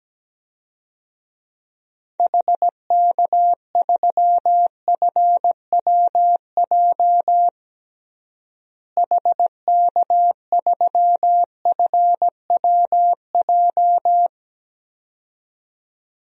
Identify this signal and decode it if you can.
Morse code, heard as a keyed tone, about 17 wpm, 700 Hz. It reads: HK3FWJ HK3FWJ
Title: DXCC-100 Call Signs - 6 Characters - Repeated 2x - 17wpm